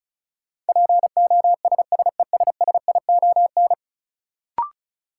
Morse code, heard as a keyed tone, about 36 wpm, 700 Hz. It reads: POSSESSION
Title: Sentences from Top 1400 Words - New Word Drill - 35wpm